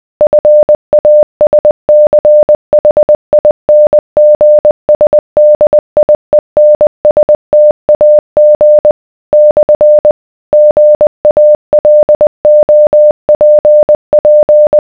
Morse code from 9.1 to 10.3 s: /